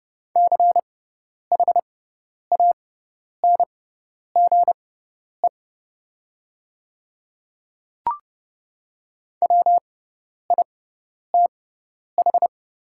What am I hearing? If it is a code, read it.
Morse code, heard as a keyed tone, about 30 wpm, spaced as for 10 wpm, 700 Hz. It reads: CHANGE WITH
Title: 6x Character Spacing - Top 200 Words - 30wpm